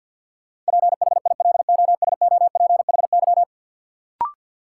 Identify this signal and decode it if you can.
Morse code, heard as a keyed tone, about 52 wpm, 700 Hz. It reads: PHILOSOPHY
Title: Sentences from Top 900 Words - New Word Drill - 50wpm